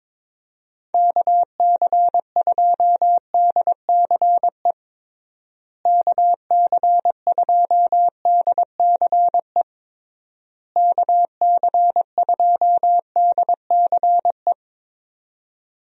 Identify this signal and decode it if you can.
Morse code, heard as a keyed tone, about 22 wpm, 700 Hz. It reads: KC2DCE KC2DCE KC2DCE
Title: USA Call Signs - 6 Characters - Repeated 3x - 22wpm